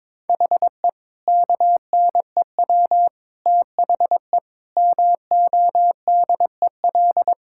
Morse code, heard as a keyed tone, about 21 wpm, 700 Hz. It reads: HE KNEW THE MODEL
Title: Sentences from Top 1000 Words - Review Part 5 - 22wpm